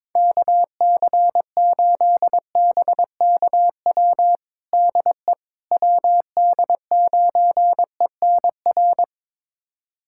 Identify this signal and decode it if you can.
Morse code, heard as a keyed tone, about 21 wpm, 700 Hz. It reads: KC8BKW DE WD9ENR
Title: USA Callsign DE Callsign - 6 Characters - 22wpm